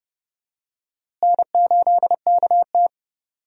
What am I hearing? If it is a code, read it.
Morse code, heard as a keyed tone, about 30 wpm, 700 Hz. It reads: N8KT
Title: USA Call Signs - 4 Characters - Repeated 2x - 30wpm